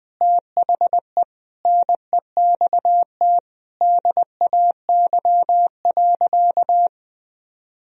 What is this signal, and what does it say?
Morse code, heard as a keyed tone, about 20 wpm, 700 Hz. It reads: THE NEXT DAY.
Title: Sentences from Top 1000 Words - Review Part 1 - 20wpm